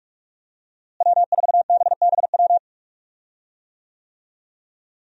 Morse code, heard as a keyed tone, about 46 wpm, 700 Hz. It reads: W4BBW
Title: USA Call Signs - 5 Characters - Repeated 3x - 45wpm